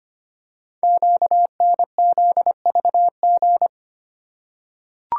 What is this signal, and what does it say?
Morse code, heard as a keyed tone, about 25 wpm, 700 Hz. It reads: QNZVG
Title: Code Groups - 5 Letters - 25wpm